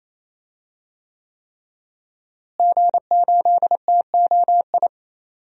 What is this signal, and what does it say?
Morse code, heard as a keyed tone, about 28 wpm, 700 Hz. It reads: G8TOI
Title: DXCC-100 Call Signs - 5 Characters - Repeated 3x - 28wpm